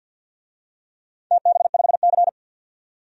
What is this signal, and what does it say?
Morse code, heard as a keyed tone, about 54 wpm, 700 Hz. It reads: TB5C
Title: DXCC-100 Call Signs - 4 Characters - 50wpm